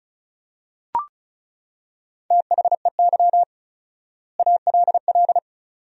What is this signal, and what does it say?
Morse code, heard as a keyed tone, about 34 wpm, 700 Hz. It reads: THEY ALL
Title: 4x Word-Spacing - Sentences from Top 500 Words - 35wpm